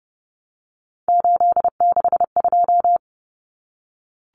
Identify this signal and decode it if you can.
Morse code, heard as a keyed tone, about 30 wpm, 700 Hz. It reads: 862
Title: Numbers - 3-Digit 30wpm